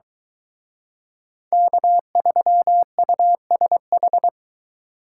Morse code, heard as a keyed tone, about 23 wpm, 700 Hz. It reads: K3USH